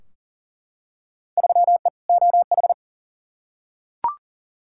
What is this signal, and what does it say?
Morse code, heard as a keyed tone, about 38 wpm, 700 Hz. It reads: 3E OH
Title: Field Day Exchanges - 40wpm